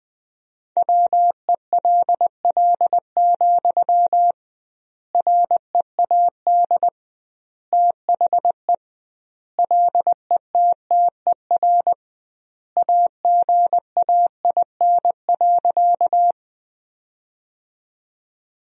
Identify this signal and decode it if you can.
Morse code, heard as a keyed tone, about 20 wpm, 700 Hz. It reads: WELL, READ THE LETTER AGAIN.
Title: 2x Word-Spacing - Sentences from Top 300 Words - 20wpm